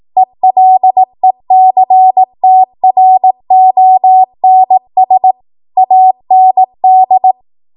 Morse code, heard as a keyed tone, about 18 wpm, 750 Hz. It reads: ELECTRONS AND